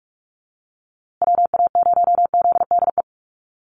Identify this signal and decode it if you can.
Morse code, heard as a keyed tone, about 44 wpm, 700 Hz. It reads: WA0ZDE
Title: USA Call Signs Repeated 3x 45wpm